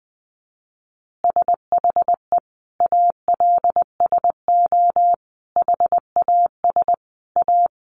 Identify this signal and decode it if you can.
Morse code, heard as a keyed tone, about 19 wpm, 700 Hz. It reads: SHE ALSO HAS A
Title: Easy Phrases 20wpm